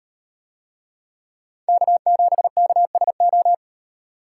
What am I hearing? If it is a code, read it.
Morse code, heard as a keyed tone, about 38 wpm, 700 Hz. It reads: K7KSO